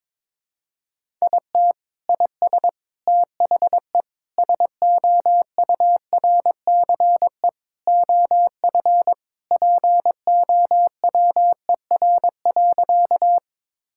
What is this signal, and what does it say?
Morse code, heard as a keyed tone, about 21 wpm, 700 Hz. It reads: IT IS THE SOURCE OF POWER.
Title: Sentences from Top 1000 Words - Review Part 5 - 22wpm